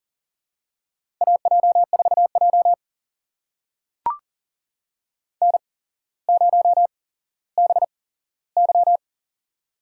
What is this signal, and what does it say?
Morse code, heard as a keyed tone, about 40 wpm, spaced as for 17 wpm, 700 Hz. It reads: AJ4J N0BY
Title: USA Call Signs - 4 Characters - 8x Character Spacing - 40wpm